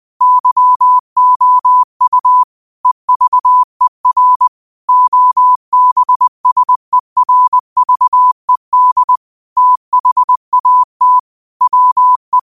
Morse code, heard as a keyed tone, about 20 wpm, 1000 Hz. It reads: YOU EVER OBSERVED THAT WE
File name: qotdfiles / 20wpm